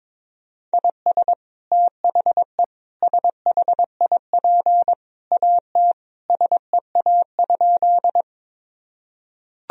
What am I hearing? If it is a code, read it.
Morse code, heard as a keyed tone, about 22 wpm, 700 Hz. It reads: IS THE SHIP AT SEA?
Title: Sentences from Top 1000 Words - Review Part 2 - 22wpm